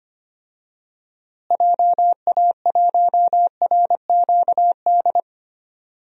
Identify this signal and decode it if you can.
Morse code, heard as a keyed tone, about 25 wpm, 700 Hz. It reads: JA1RQD
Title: DXCC-100 Call Signs - 6 Characters - Repeated 3x - 25wpm